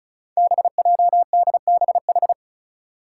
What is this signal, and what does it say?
Morse code, heard as a keyed tone, about 36 wpm, 700 Hz. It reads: BJDBH
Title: Five Challenging Letter-Number Combinations 35wpm